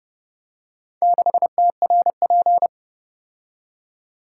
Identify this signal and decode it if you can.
Morse code, heard as a keyed tone, about 29 wpm, 700 Hz. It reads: 6TRP